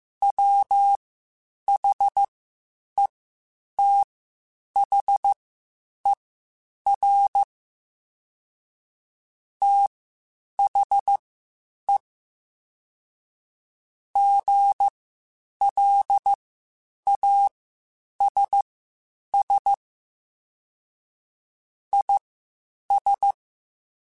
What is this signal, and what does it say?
Morse code, heard as a keyed tone, about 15 wpm, spaced as for 8 wpm, 800 Hz. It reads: WHETHER THE GLASS IS
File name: Spaced / Chuckles Spaced-mp3s / HalfFull sp @13